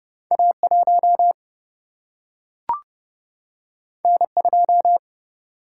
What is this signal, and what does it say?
Morse code, heard as a keyed tone, about 30 wpm, 700 Hz. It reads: A 1 N 2
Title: Single-Letter Single-Number 30wpm